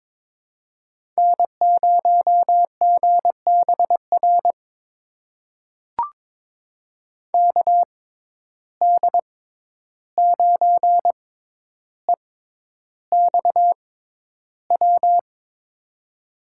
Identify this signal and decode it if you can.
Morse code, heard as a keyed tone, about 21 wpm, spaced as for 11 wpm, 700 Hz. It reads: N0GBR KD9EXW
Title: USA Call Signs - 6x Character Spacing - 22wpm